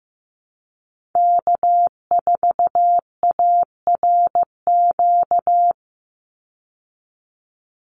Morse code, heard as a keyed tone, about 15 wpm, 700 Hz. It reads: K4ARQ
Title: Call Signs Repeated 3x - 15wpm